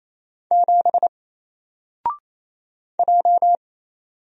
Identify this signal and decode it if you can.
Morse code, heard as a keyed tone, about 28 wpm, 700 Hz. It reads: 7 J
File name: Single-Letter-Number-28wpm